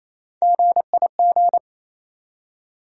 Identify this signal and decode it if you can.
Morse code, heard as a keyed tone, about 27 wpm, 700 Hz. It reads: GIG
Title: USA Call Sign Postfixes - 3 Letters 28wpm